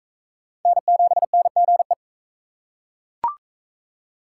Morse code, heard as a keyed tone, about 40 wpm, 700 Hz. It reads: N7NGE